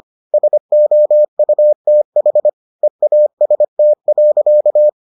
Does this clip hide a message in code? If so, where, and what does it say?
Yes: Morse code from 0.3 to 4.9 s, heard as a keyed tone, about 25 wpm, 600 Hz. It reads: SOUTH EAST.